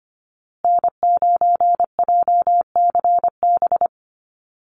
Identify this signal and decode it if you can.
Morse code, heard as a keyed tone, about 25 wpm, 700 Hz. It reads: N9JCB
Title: USA Call Signs Repeated 2x 25wpm